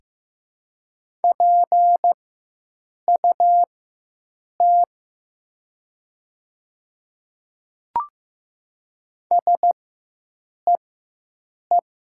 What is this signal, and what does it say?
Morse code, heard as a keyed tone, about 15 wpm, spaced as for 7 wpm, 700 Hz. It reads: PUT SEE
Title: 4x Character Spacing - Top 200 Words - 15wpm